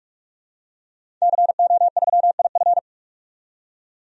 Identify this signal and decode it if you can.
Morse code, heard as a keyed tone, about 47 wpm, 700 Hz. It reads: CO3IF
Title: DXCC-100 Call Signs - 5 Characters - 45wpm